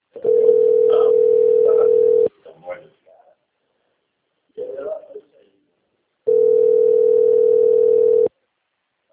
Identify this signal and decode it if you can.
Ringback tone (North America), 440 + 480 Hz, with speech.
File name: rg-1600-unknown- - - .3268